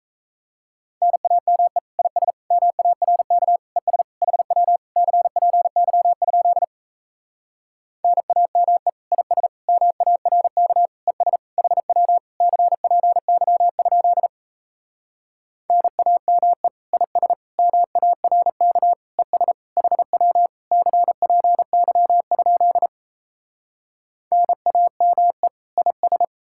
Morse code, heard as a keyed tone, about 38 wpm, 700 Hz. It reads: NAME IS MARK ES HW CPY? NAME IS MARK ES HW CPY? NAME IS MARK ES HW CPY? NAME IS